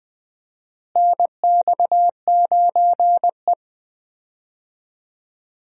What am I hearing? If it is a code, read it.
Morse code, heard as a keyed tone, about 20 wpm, 700 Hz. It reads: NX9E
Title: USA Call Signs - 4 Characters - Repeated 3x - 20wpm